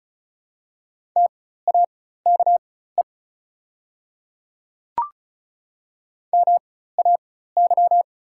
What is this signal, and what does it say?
Morse code, heard as a keyed tone, about 36 wpm, spaced as for 16 wpm, 700 Hz. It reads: TAKE MAY
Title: 4x Character Spacing - Top 200 Words - 35wpm